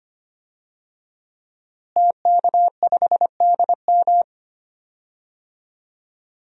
Morse code, heard as a keyed tone, about 25 wpm, 700 Hz. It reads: TK5DM
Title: DXCC-100 Call Signs - 5 Characters - Repeated 2x - 25wpm